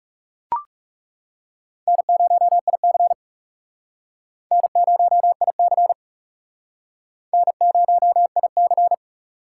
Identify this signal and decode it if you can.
Morse code, heard as a keyed tone, about 37 wpm, 700 Hz. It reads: N0IC N0IC N0IC
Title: Speed Racing - USA 1x2 Call Signs 30wpm